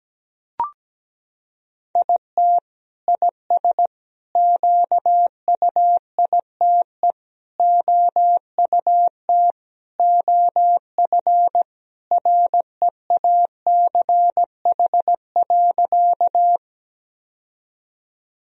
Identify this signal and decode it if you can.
Morse code, heard as a keyed tone, about 17 wpm, 700 Hz. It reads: IT IS QUITE OUT OF REACH.
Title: Sentences from Top 1000 Words - Review Part 2 - 17wpm